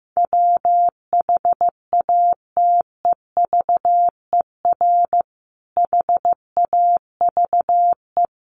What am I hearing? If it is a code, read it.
Morse code, heard as a keyed tone, about 15 wpm, 700 Hz. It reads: WHATEVER HAVE
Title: Sentences from Top 1100 Words 15wpm